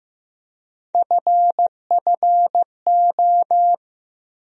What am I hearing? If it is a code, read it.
Morse code, heard as a keyed tone, about 15 wpm, 700 Hz. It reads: FFO
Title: USA Call Sign Postfixes - 3 Letters 15wpm